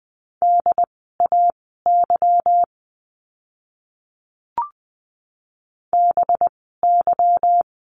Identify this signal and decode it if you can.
Morse code, heard as a keyed tone, about 20 wpm, spaced as for 14 wpm, 700 Hz. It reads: DAY BY
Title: 206 Lesson 33f - Letter j - Words 20wpm